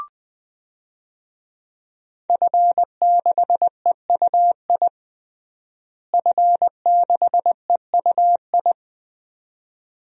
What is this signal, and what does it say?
Morse code, heard as a keyed tone, about 20 wpm, 700 Hz. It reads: F6EUI F6EUI
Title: DXCC-100 Call Signs - 5 Characters - Repeated 2x - 20wpm